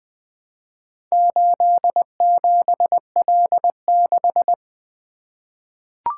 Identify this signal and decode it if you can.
Morse code, heard as a keyed tone, about 21 wpm, 700 Hz. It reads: 87L6
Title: Four Letter-Number Combinations - 20wpm